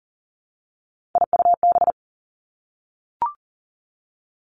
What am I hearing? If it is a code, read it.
Morse code, heard as a keyed tone, about 40 wpm, 700 Hz. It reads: IUB
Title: Three Challenging Letter-Number Combinations 40wpm